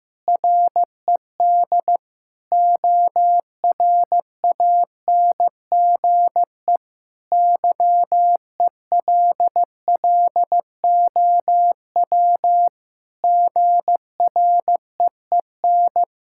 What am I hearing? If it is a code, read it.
Morse code, heard as a keyed tone, about 16 wpm, 700 Hz. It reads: RED ORANGE YELLOW GREEN
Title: Pangram Challenge v2 - 15wpm